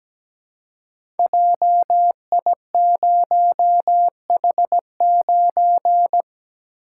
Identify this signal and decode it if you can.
Morse code, heard as a keyed tone, about 17 wpm, 700 Hz. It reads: JI0H9
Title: Code Groups - 5 Letter:Numbers - 17wpm